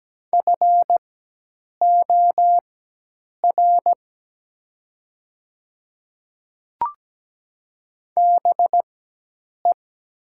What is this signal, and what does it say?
Morse code, heard as a keyed tone, about 17 wpm, spaced as for 8 wpm, 700 Hz. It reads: FOR BE
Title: 4x Character Spacing - Top 200 Words - 17wpm